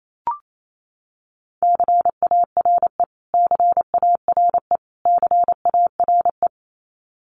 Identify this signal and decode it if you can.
Morse code, heard as a keyed tone, about 28 wpm, 700 Hz. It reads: CARE CARE CARE
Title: Top 300 Words Repeated 28wpm